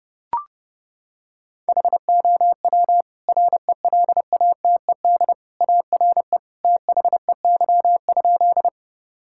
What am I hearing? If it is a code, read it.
Morse code, heard as a keyed tone, about 32 wpm, 700 Hz. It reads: HOW RELATED ARE THEY?
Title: Sentences from Top 1000 Words - Review Part 5 - 30wpm